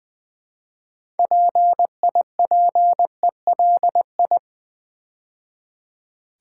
Morse code, heard as a keyed tone, about 20 wpm, 700 Hz. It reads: PIPELI
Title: Build A Word - 2 Letters At A Time - 8-Letter Words - 20wpm